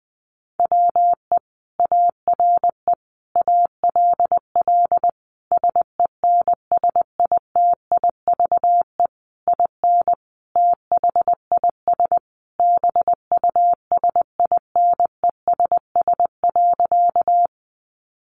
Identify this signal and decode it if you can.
Morse code, heard as a keyed tone, about 20 wpm, 700 Hz. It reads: WE ARE ALL SENSITIVE IN THIS BUSINESS.
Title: Sentences from Top 1400 Words 20wpm